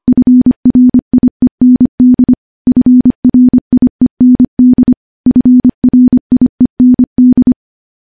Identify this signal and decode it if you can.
Morse code, heard as a keyed tone, about 25 wpm, 250 Hz. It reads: FRIEND FRIEND FRIEND
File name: morse / morse-training-25WPM